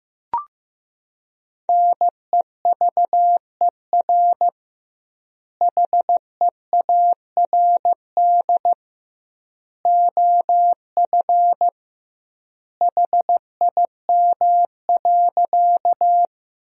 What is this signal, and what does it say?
Morse code, heard as a keyed tone, about 15 wpm, 700 Hz. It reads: NEVER HEARD OF HIM.
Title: 2x Word-Spacing - Sentences from Top 400 Words - 15wpm